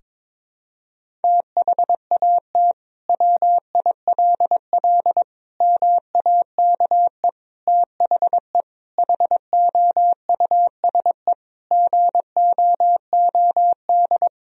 Morse code, heard as a keyed tone, about 22 wpm, 700 Hz. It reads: THAT WILL MAKE THE HOUSE GOOD
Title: Sentences from Top 1000 Words - Review Part 3 - 22wpm